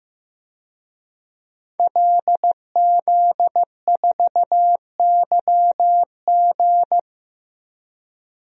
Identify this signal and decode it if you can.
Morse code, heard as a keyed tone, about 15 wpm, 700 Hz. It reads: LZ4YG
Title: DXCC-100 Call Signs - 5 Characters - 15wpm